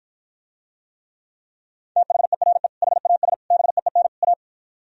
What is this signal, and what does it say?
Morse code, heard as a keyed tone, about 56 wpm, 700 Hz. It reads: THERE HAS BEEN A